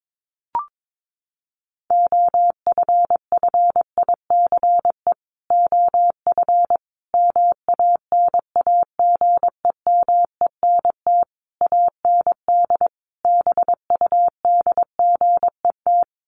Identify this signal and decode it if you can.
Morse code, heard as a keyed tone, about 22 wpm, 700 Hz. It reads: OFFICE OF MANAGEMENT AND BUDGET
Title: Sets of 5 Words Encore2 22wpm